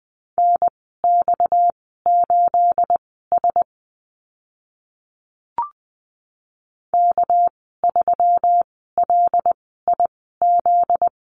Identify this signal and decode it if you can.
Morse code, heard as a keyed tone, about 20 wpm, spaced as for 15 wpm, 700 Hz. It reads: NX8S K3LIZ
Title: 245 Lesson 39g - Letter z - Call Signs 20wpm